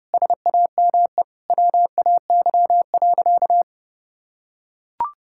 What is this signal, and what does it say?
Morse code, heard as a keyed tone, about 30 wpm, 700 Hz. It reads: SAME WAY.
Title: Sentences from Top 1600 Words 30wpm